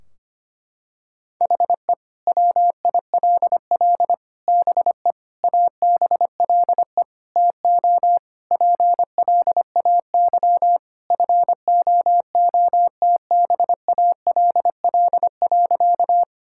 Morse code, heard as a keyed tone, about 25 wpm, 700 Hz. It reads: HE WILL BE ABLE TO PLAY FOOTBALL.
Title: Sentences from Top 1600 Words 25wpm